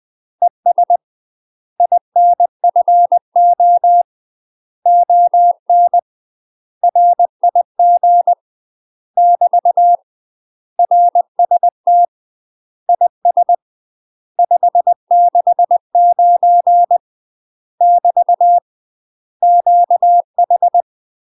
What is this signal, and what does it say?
Morse code, heard as a keyed tone, about 20 wpm, 700 Hz. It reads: ES INFO ON RIG <BT> RST IS 569 <BT> QH
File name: UK MASTER QSOs TRAINING/20WPM